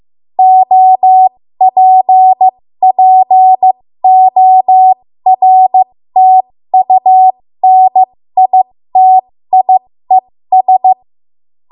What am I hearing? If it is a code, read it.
Morse code, heard as a keyed tone, about 15 wpm, spaced as for 13 wpm, 750 Hz. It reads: OPPORTUNITIES
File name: Archive / 13 WPM / 13WPM